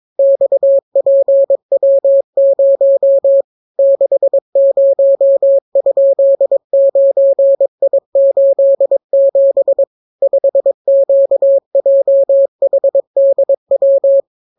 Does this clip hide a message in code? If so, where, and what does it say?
Yes: Morse code from 0.2 to 14.2 s, heard as a keyed tone, about 22 wpm, 550 Hz. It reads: XPW0 60?9I87 5QJHDW